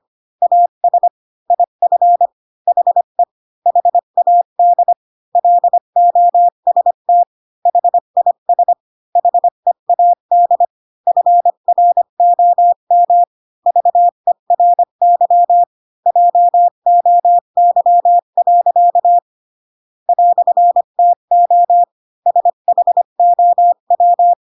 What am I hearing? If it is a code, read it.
Morse code, heard as a keyed tone, about 25 wpm, 700 Hz. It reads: AS IF HE HAD LOST HIS HEAD FROM VERY JOY. "TO SHOW
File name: Pinocchio / Pinocchio ch8 20 25 ik1hge